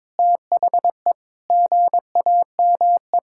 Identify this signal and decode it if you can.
Morse code, heard as a keyed tone, about 21 wpm, 700 Hz. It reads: THE GAME
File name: practice / 22wpm / Sentences with 2-Word Phrases - 22wpm